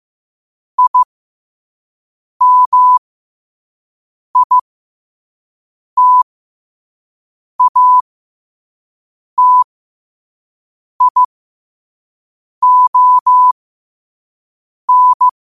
Morse code, heard as a keyed tone, about 15 wpm, spaced as for 5 wpm, 1000 Hz. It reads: IMITATION
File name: qotdfiles / 5wpm